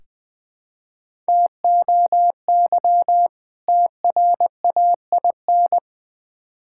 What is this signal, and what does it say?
Morse code, heard as a keyed tone, about 20 wpm, 700 Hz. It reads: TOY TRAIN
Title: Difficult Words with Hints - 20wpm